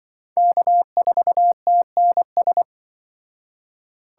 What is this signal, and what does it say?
Morse code, heard as a keyed tone, about 24 wpm, 700 Hz. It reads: K4TNS
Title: Speed Racing - USA 1x3 Call Signs 20wpm